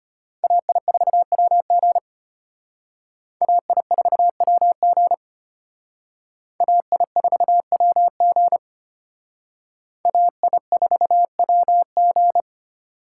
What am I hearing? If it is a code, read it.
Morse code, heard as a keyed tone, about 33 wpm, 700 Hz. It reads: AI4WG AI4WG AI4WG AI4WG